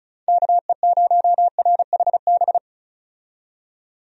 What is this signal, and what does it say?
Morse code, heard as a keyed tone, about 34 wpm, 700 Hz. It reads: KE0RHB
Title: USA Call Signs - 6 Characters - 35wpm